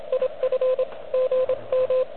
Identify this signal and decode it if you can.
Morse code, heard as a keyed tone, about 24 wpm, 510 Hz, with radio noise. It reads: IF GM
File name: OK QRP 2022 / Zvuk / REC36